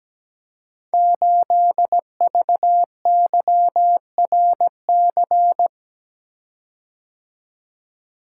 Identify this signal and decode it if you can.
Morse code, heard as a keyed tone, about 17 wpm, 700 Hz. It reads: 8VYRC
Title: Code Groups - 5 Letter:Numbers - 17wpm